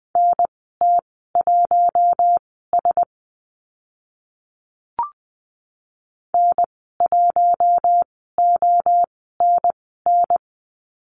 Morse code, heard as a keyed tone, about 20 wpm, spaced as for 14 wpm, 700 Hz. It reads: NT1S N1ONN
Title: 042 Lesson 8g - Number 1 - Call Signs 20wpm